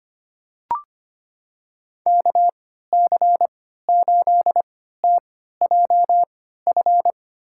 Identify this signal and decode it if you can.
Morse code, heard as a keyed tone, about 25 wpm, spaced as for 14 wpm, 700 Hz. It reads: KC8TJF
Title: USA Call Signs - 6 Characters - 3x Character Spacing - 25wpm